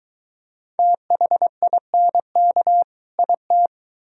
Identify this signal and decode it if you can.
Morse code, heard as a keyed tone, about 22 wpm, 700 Hz. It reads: THINK IT